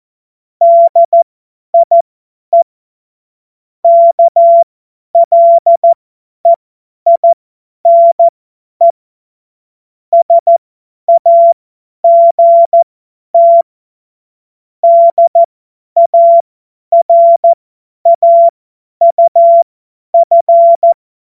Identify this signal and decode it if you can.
Morse code, heard as a keyed tone, about 14 wpm, spaced as for 10 wpm, 680 Hz. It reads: DIE KLEINE SAGT DARAUF
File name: Morsekurs / Witze / Witz 07. März 2023 50 F70